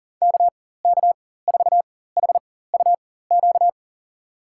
Morse code, heard as a keyed tone, about 40 wpm, spaced as for 19 wpm, 700 Hz. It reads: KK4HUQ
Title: USA Call Signs - 6 Characters - 4x Character Spacing - 40wpm